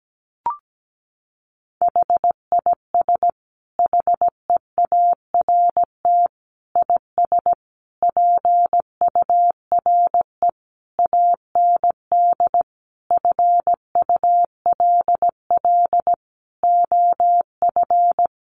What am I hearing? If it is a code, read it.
Morse code, heard as a keyed tone, about 17 wpm, 700 Hz. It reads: HIS HEART IS PURE AND FULL OF
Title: Sentences from Top 1500 Words 17wpm